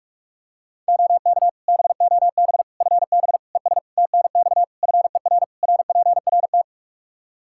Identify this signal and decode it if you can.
Morse code, heard as a keyed tone, about 46 wpm, 700 Hz. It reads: OK BOB FB ES TNX FER RPRT
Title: Practice QSO - Exchange 2 - Info - 45wpm